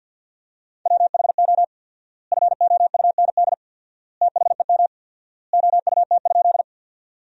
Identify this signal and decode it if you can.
Morse code, heard as a keyed tone, about 50 wpm, 700 Hz. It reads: WHO FOUND THEM OUT?
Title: 4x Word-Spacing - Sentences from Top 300 Words - 50wpm